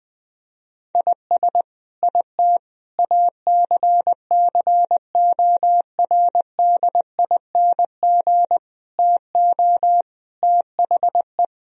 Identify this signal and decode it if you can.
Morse code, heard as a keyed tone, about 21 wpm, 700 Hz. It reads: IS IT ACCORDING TO THE
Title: Sentences from Top 1000 Words - Review Part 4 - 20wpm